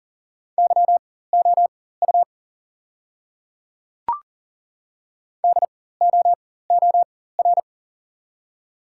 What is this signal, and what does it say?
Morse code, heard as a keyed tone, about 39 wpm, spaced as for 18 wpm, 690 Hz. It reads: YOU DOOR
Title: 4x Character Spacing - Top 300 Words - 40wpm